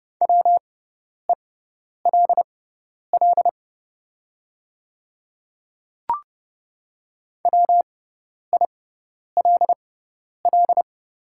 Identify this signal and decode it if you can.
Morse code, heard as a keyed tone, about 30 wpm, spaced as for 10 wpm, 690 Hz. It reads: WELL WILL
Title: 6x Character Spacing - Top 200 Words - 30wpm